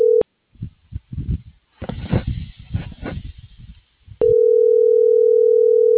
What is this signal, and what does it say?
Ringback tone (North America), 440 + 480 Hz.